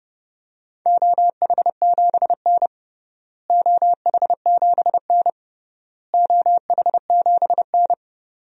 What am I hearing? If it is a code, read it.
Morse code, heard as a keyed tone, about 30 wpm, 700 Hz. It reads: OH7N OH7N OH7N
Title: DXCC-100 Call Signs - 4 Characters - Repeated 3x - 30wpm